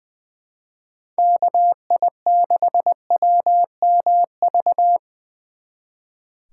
Morse code, heard as a keyed tone, about 24 wpm, 700 Hz. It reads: KI6WMV
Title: Speed Racing - USA 2x3 Call Signs 20wpm